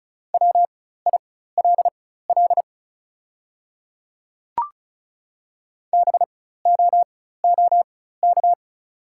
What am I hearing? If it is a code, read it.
Morse code, heard as a keyed tone, about 36 wpm, spaced as for 16 wpm, 700 Hz. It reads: WILL BOOK
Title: 4x Character Spacing - Top 300 Words - 35wpm